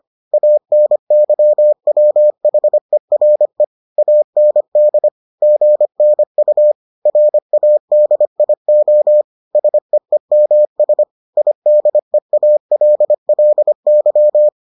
Morse code, heard as a keyed tone, about 25 wpm, 600 Hz. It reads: ANYWHERE AND GNU RADIO SEEMS IDEALLY